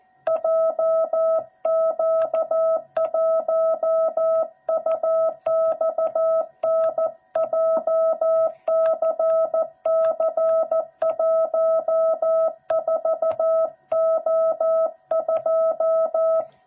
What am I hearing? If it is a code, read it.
Morse code, heard as a keyed tone, about 14 wpm, 660 Hz. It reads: JQ1UXN JCC14O2
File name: jh8cbh / 2021cw / cw06b